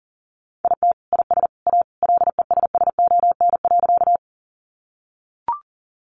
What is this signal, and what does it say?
Morse code, heard as a keyed tone, about 40 wpm, 700 Hz. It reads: IT IS A LESSON.
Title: Sentences from Top 1200 Words 40wpm